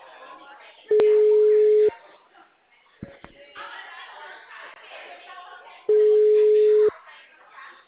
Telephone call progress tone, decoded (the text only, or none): ringback tone